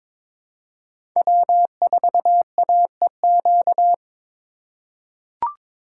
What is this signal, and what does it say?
Morse code, heard as a keyed tone, about 22 wpm, 700 Hz. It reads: W4AEQ
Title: USA Call Signs - 5 Characters - 2x Character Spacing - 22wpm